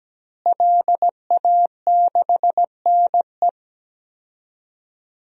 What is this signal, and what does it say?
Morse code, heard as a keyed tone, about 17 wpm, 700 Hz. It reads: LA6NE
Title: DXCC-100 Call Signs - 5 Characters - Repeated 3x - 17wpm